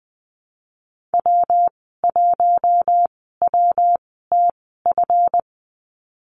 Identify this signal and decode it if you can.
Morse code, heard as a keyed tone, about 19 wpm, spaced as for 14 wpm, 700 Hz. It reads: W1WTF
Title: 179 Lesson 29g - Slash - Call Signs 20wpm